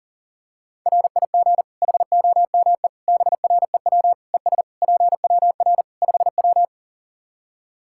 Morse code, heard as a keyed tone, about 40 wpm, 700 Hz. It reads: RIG HOME BREW ES PWR 5W
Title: K1USN SST QSO Practice 40wpm